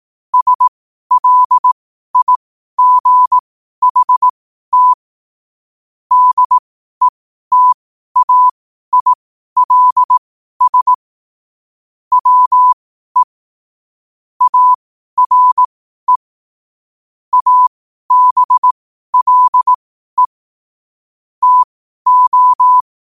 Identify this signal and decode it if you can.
Morse code, heard as a keyed tone, about 18 wpm, spaced as for 12 wpm, 1000 Hz. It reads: SLIGHT DETAILS WE ARE ABLE TO
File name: 12wpm